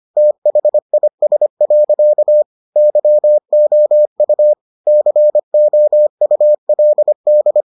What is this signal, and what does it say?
Morse code, heard as a keyed tone, about 25 wpm, 600 Hz. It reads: THIS. YOU COULD